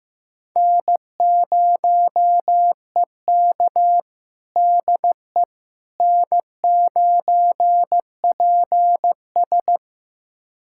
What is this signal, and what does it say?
Morse code, heard as a keyed tone, about 15 wpm, 700 Hz. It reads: N0EK DE N9PS
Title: USA Callsign DE Callsign - 4 Characters - 15wpm